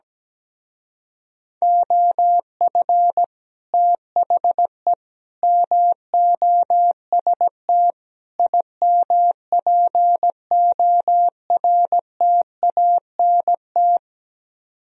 Morse code, heard as a keyed tone, about 17 wpm, 700 Hz. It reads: OF THE MOST IMPORTANT
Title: Speed-Racing - Sets of 4 Words 17wpm